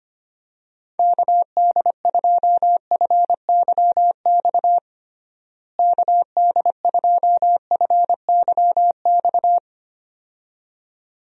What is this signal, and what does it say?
Morse code, heard as a keyed tone, about 25 wpm, 700 Hz. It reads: KD2FYX KD2FYX
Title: USA Call Signs - 6 Characters - Repeated 2x - 25wpm